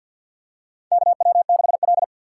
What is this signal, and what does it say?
Morse code, heard as a keyed tone, about 51 wpm, 700 Hz. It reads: KW6L
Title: USA Call Signs - 4 Characters - 3x Character Spacing - 50wpm